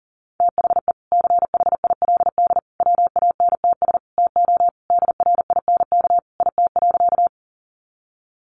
Morse code, heard as a keyed tone, about 40 wpm, 700 Hz. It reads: THE CHILD WANTS TO DRINK IT.
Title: Additional Easy Phrases 40wpm